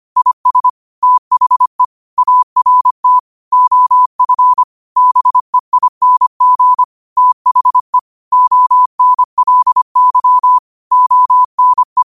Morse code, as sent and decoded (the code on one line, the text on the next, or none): .. ... / - .... / .- .-. - / --- ..-. / -... .. -. --. / - .... / --- -. .-.. -.-- / --- -. .
IS THE ART OF BEING THE ONLY ONE